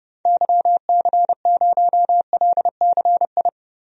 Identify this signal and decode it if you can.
Morse code, heard as a keyed tone, about 30 wpm, 700 Hz. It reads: YC0LCI
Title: DXCC-100 Call Signs - 6 Characters - Repeated 3x - 30wpm